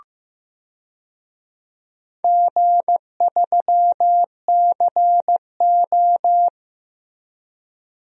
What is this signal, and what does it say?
Morse code, heard as a keyed tone, about 15 wpm, 700 Hz. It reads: G3CO
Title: DXCC-100 Call Signs - 4 Characters - Repeated 2x - 15wpm